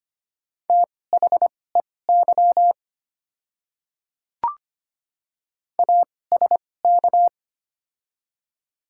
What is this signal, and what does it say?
Morse code, heard as a keyed tone, about 25 wpm, spaced as for 18 wpm, 690 Hz. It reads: THEY ASK
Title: 2x Character Spacing - Top 200 Words - 25wpm